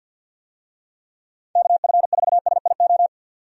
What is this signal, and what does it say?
Morse code, heard as a keyed tone, about 52 wpm, 700 Hz. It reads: KF4SIO
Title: USA Call Signs - 6 Characters - 50wpm